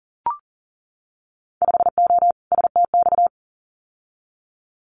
Morse code, heard as a keyed tone, about 40 wpm, 700 Hz. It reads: 5O STX
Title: Winter Field Day Exchanges 40wpm